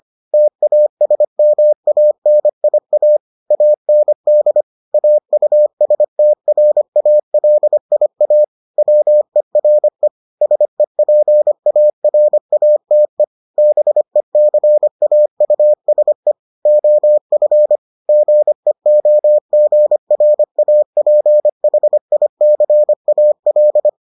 Morse code, heard as a keyed tone, about 25 wpm, 600 Hz. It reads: TASMANIA AND AUSTRALIA WERE SEPARATE BECAUSE OF GEOGRAPHICAL